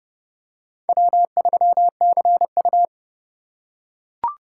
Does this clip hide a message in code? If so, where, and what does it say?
Yes: Morse code from 0.9 to 2.9 s, heard as a keyed tone, about 30 wpm, 700 Hz. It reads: W3CU